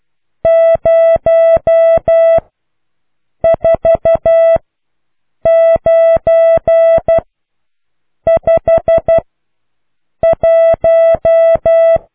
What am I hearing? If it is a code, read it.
Morse code, heard as a keyed tone, about 12 wpm, spaced as for 6 wpm, 660 Hz. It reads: 04951